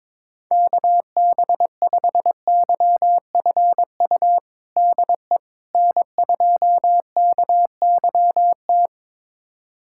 Morse code, heard as a keyed tone, about 21 wpm, 700 Hz. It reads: KB5YFU DE N2KYT
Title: USA Callsign DE Callsign 22wpm